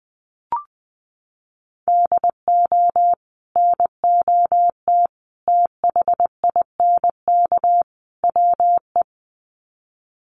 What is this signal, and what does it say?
Morse code, heard as a keyed tone, about 20 wpm, 700 Hz. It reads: DO NOT THINK WE
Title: Sets of 4 Words 20wpm